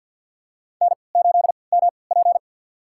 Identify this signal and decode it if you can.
Morse code, heard as a keyed tone, about 48 wpm, spaced as for 28 wpm, 700 Hz. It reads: N8MP
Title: USA Call Signs - 4 Characters - 3x Character Spacing - 50wpm